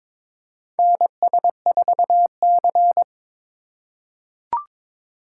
Morse code, heard as a keyed tone, about 22 wpm, 700 Hz. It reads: NS4C
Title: USA Call Signs - 3x Character Spacing - 22wpm